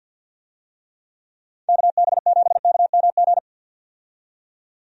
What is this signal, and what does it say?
Morse code, heard as a keyed tone, about 51 wpm, 700 Hz. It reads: KB7KMG
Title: USA Call Signs - 6 Characters - 50wpm